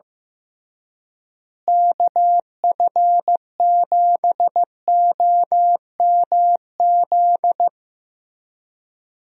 Morse code, heard as a keyed tone, about 15 wpm, 700 Hz. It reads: KF7OMZ
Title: USA Call Signs - 6 Characters - Repeated 2x - 15wpm